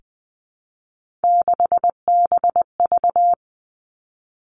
Morse code, heard as a keyed tone, about 20 wpm, 700 Hz. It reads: 6BV
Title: Three Challenging Letter-Number Combinations 20wpm